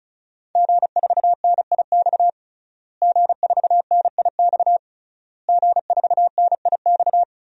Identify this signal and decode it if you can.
Morse code, heard as a keyed tone, about 36 wpm, 700 Hz. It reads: G4NIX G4NIX G4NIX
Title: DXCC-100 Call Signs - 5 Characters - Repeated 3x - 35wpm